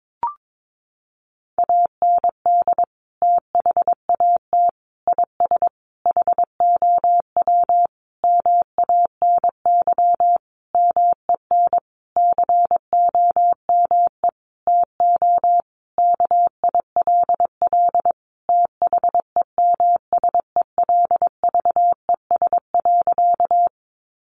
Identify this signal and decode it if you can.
Morse code, heard as a keyed tone, about 22 wpm, 700 Hz. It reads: AND THAT IS HOW MANY MEN COME TO KILL THEMSELVES.
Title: Sentences from Top 1400 Words 22wpm